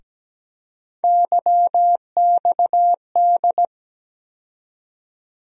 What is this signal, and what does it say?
Morse code, heard as a keyed tone, about 17 wpm, 700 Hz. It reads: YXD
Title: USA Call Sign Postfixes - 3 Letters 17wpm